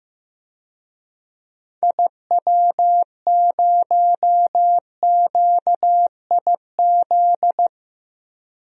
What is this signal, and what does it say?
Morse code, heard as a keyed tone, about 15 wpm, 700 Hz. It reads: IW0QIZ
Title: DXCC-100 Call Signs - 6 Characters - Repeated 3x - 15wpm